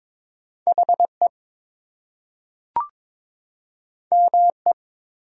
Morse code, heard as a keyed tone, about 22 wpm, 700 Hz. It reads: HE ME
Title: Two Letter Words - 22wpm